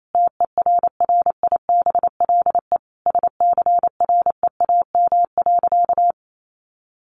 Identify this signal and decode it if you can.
Morse code, heard as a keyed tone, about 28 wpm, 700 Hz. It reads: TERRIBLE SCREAM.
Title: A Sound Of Thunder 28wpm